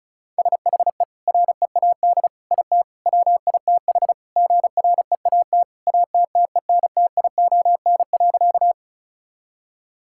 Morse code, heard as a keyed tone, about 36 wpm, 700 Hz. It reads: SHE READ IT WITH GREAT ATTENTION.
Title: Sentences from Top 1000 Words - Review Part 6 - 35wpm